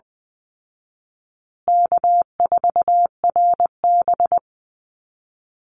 Morse code, heard as a keyed tone, about 20 wpm, 700 Hz. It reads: K4RB
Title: Easy Call Signs Repeated 3x - 20wpm